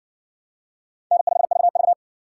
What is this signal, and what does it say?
Morse code, heard as a keyed tone, about 52 wpm, 700 Hz. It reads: N5VV